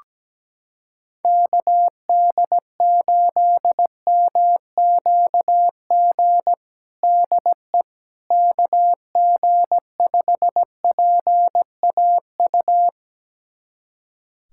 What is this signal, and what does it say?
Morse code, heard as a keyed tone, about 17 wpm, 700 Hz. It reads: KD8MQG DE KG5PAU
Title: USA Callsign DE Callsign 17wpm